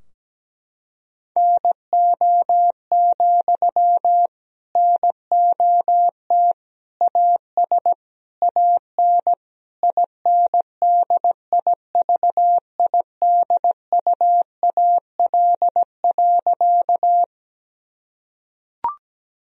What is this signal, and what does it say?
Morse code, heard as a keyed tone, about 17 wpm, 700 Hz. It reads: NO, NOT AS AN INDIVIDUAL.
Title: Sentences from Top 1000 Words - Review Part 4 - 17wpm